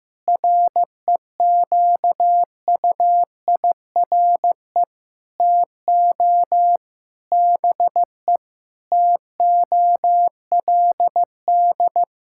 Morse code, heard as a keyed tone, about 15 wpm, 700 Hz. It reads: REQUIRE TO BE TOLD
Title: Sentences from Top 1600 Words 15wpm